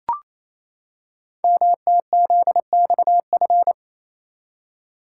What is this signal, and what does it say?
Morse code, heard as a keyed tone, about 29 wpm, 700 Hz. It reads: MTZXF
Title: Code Groups - 5 Letters - 28wpm